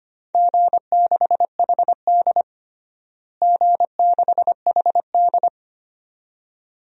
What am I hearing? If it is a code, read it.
Morse code, heard as a keyed tone, about 25 wpm, 700 Hz. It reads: G6HD G6HD
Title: DXCC-100 Call Signs - 4 Characters - Repeated 2x - 25wpm